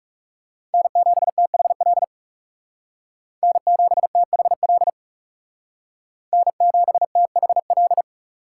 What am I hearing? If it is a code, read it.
Morse code, heard as a keyed tone, about 36 wpm, 700 Hz. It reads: N7THL N7THL N7THL
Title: Speed Racing - USA 1x3 Call Signs 30wpm